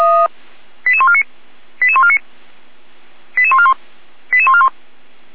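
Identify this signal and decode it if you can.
Five-tone selective call (ZVEI), heard as quick runs of five tones, tones of about 70 ms. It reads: 88149 88149 88141 88141
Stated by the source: SDR receiver recording of ZVEI five-tone selective call, code 88216.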